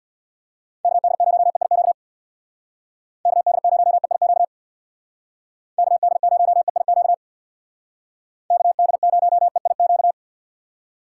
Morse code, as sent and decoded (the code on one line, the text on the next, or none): -.- -.. ----- . .. --.- / -.- -.. ----- . .. --.- / -.- -.. ----- . .. --.- / -.- -.. ----- . .. --.-
KD0EIQ KD0EIQ KD0EIQ KD0EIQ